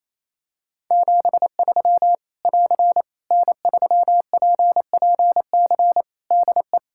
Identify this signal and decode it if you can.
Morse code, heard as a keyed tone, about 27 wpm, 700 Hz. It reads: 73 <AR> N3PPC DE
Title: Practice QSO - Exchange 4 - Ending - 28wpm